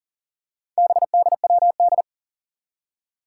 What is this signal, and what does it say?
Morse code, heard as a keyed tone, about 40 wpm, 700 Hz. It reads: BDWD
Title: Four Challenging Letter-Number Combinations 40wpm